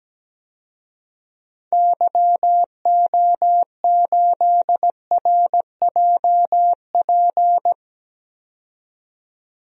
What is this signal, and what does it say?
Morse code, heard as a keyed tone, about 17 wpm, 700 Hz. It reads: YO8RJP
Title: DXCC-100 Call Signs - 6 Characters - 17wpm